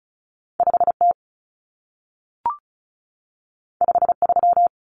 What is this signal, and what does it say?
Morse code, heard as a keyed tone, about 36 wpm, 700 Hz. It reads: 5T 53
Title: Common Cut Numbers - 2-Digit 35wpm